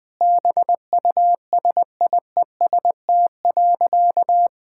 Morse code, heard as a keyed tone, about 20 wpm, 700 Hz. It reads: BUSIEST.
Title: Sentences with 7-Word Phrases - 20wpm